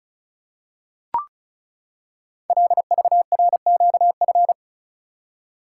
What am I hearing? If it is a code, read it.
Morse code, heard as a keyed tone, about 36 wpm, 700 Hz. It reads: LVRQF
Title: Code Groups - 5 Letters - 35wpm